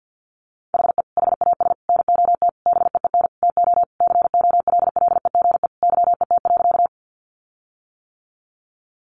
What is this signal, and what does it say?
Morse code, heard as a keyed tone, about 50 wpm, 700 Hz. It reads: HE HAS NOT BEEN TO COLLEGE YET.
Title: Sentences from Top 1300 Words 50wpm